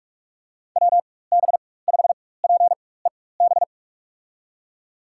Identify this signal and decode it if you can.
Morse code, heard as a keyed tone, about 46 wpm, spaced as for 21 wpm, 700 Hz. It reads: WB5PEB
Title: USA Call Signs - 6 Characters - 4x Character Spacing - 45wpm